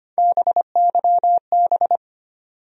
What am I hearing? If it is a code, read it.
Morse code, heard as a keyed tone, about 25 wpm, 700 Hz. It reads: BYB
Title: USA Call Sign Postfixes - 3 Letters 25wpm